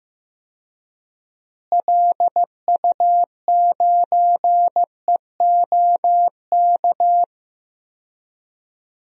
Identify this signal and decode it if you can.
Morse code, heard as a keyed tone, about 15 wpm, 700 Hz. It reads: LU9EOK
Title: DXCC-100 Call Signs - 6 Characters - Repeated 2x - 15wpm